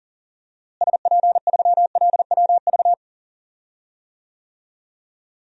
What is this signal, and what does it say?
Morse code, heard as a keyed tone, about 40 wpm, 700 Hz. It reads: SP3LWV
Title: DXCC-100 Call Signs - 6 Characters - Repeated 2x - 40wpm